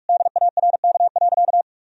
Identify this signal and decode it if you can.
Morse code, heard as a keyed tone, about 46 wpm, 700 Hz. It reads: DARK.
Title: Sentences from Top 1000 Words - Review Part 2 - 45wpm